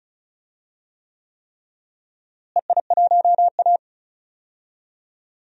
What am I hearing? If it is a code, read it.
Morse code, heard as a keyed tone, about 34 wpm, 700 Hz. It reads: EI1A
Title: DXCC-100 Call Signs - 4 Characters - Repeated 3x - 35wpm